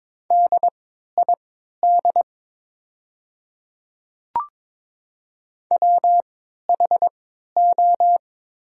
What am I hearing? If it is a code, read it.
Morse code, heard as a keyed tone, about 22 wpm, spaced as for 13 wpm, 700 Hz. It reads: DID WHO
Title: 3x Character Spacing - Top 100 Words - 22wpm